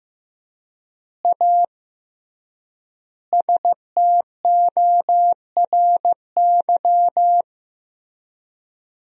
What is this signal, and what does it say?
Morse code, heard as a keyed tone, about 15 wpm, 700 Hz. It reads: A STORY
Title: 3x Word-Spacing - Sentences from Top 500 Words - 15wpm